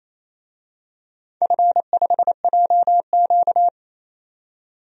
Morse code, heard as a keyed tone, about 27 wpm, 700 Hz. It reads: F5JQ
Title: DXCC-100 Call Signs - 4 Characters - 28wpm